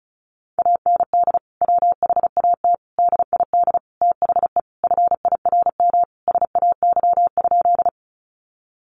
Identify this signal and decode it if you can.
Morse code, heard as a keyed tone, about 34 wpm, 700 Hz. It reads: AND WHAT DID THE FIRM SAY?
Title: Sentences from Top 1400 Words 35wpm